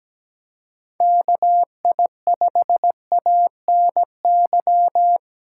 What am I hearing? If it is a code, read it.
Morse code, heard as a keyed tone, about 17 wpm, 700 Hz. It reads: KI5ANY
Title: USA Callsign DE Callsign 17wpm